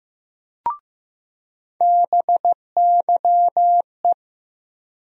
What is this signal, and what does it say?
Morse code, heard as a keyed tone, about 15 wpm, 700 Hz. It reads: BYE
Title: USA Call Sign Postfixes - 3 Letters 15wpm